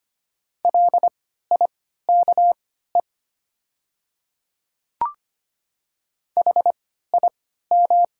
Morse code, heard as a keyed tone, about 25 wpm, spaced as for 14 wpm, 700 Hz. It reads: LIKE HIM
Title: 3x Character Spacing - Top 100 Words - 25wpm